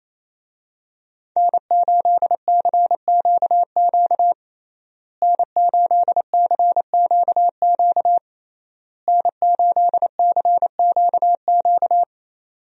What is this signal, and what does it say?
Morse code, heard as a keyed tone, about 28 wpm, 700 Hz. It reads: N8CQQ N8CQQ N8CQQ
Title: USA Call Signs - 5 Characters - Repeated 3x - 28wpm